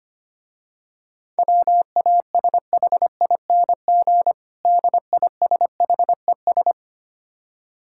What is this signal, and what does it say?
Morse code, heard as a keyed tone, about 25 wpm, 700 Hz. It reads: WASHING DISHES
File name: Easy Words - Speed-Racing with Hints - 25wpm